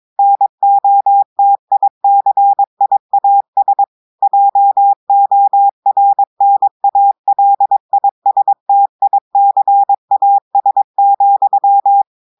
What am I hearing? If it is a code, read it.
Morse code, heard as a keyed tone, about 22 wpm, 800 Hz. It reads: NOTICIAS JORNALISTICAS,
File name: audios / portugais / portugais texte en clair 2 vitesse 22 mots minutes